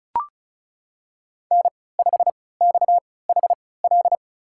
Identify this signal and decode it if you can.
Morse code, heard as a keyed tone, about 34 wpm, spaced as for 20 wpm, 700 Hz. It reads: N5XHL
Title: USA Call Signs - 5 Characters - 3x Character Spacing - 35wpm